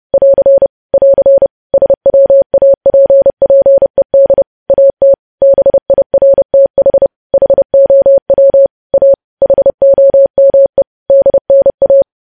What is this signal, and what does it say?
Morse code, heard as a keyed tone, about 30 wpm, 550 Hz. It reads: <AR> <AR> SWAPPED AT BIRTH HOW A HOME DNA
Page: news headlines in morse code at 30 wpm